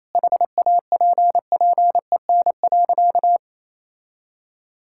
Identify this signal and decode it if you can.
Morse code, heard as a keyed tone, about 28 wpm, 700 Hz. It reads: HAPPEN.
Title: Sentences from Top 1000 Words - Review Part 6 - 28wpm